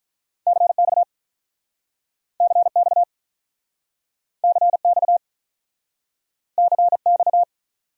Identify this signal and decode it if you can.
Morse code, heard as a keyed tone, about 46 wpm, 700 Hz. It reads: CX CX CX CX